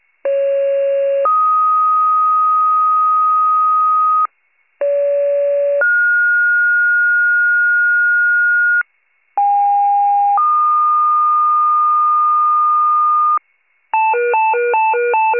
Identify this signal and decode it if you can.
Two-tone page, heard as a one-second tone, then a three-second tone, about 550 then 1250 Hz, about 570 then 1430 Hz, about 810 then 1220 Hz. It is heard from 0.2 to 13.4 s.